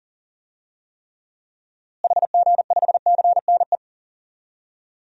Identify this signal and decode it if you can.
Morse code, heard as a keyed tone, about 40 wpm, 700 Hz. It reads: HG5CNE
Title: DXCC-100 Call Signs - 6 Characters - Repeated 2x - 40wpm